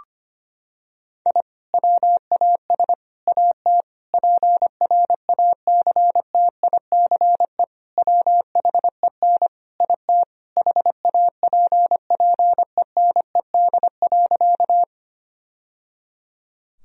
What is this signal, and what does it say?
Morse code, heard as a keyed tone, about 25 wpm, 700 Hz. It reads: I WAS AT PRACTICE WHEN IT HAPPENED.
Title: Sentences from Top 1000 Words - Review Part 4 - 25wpm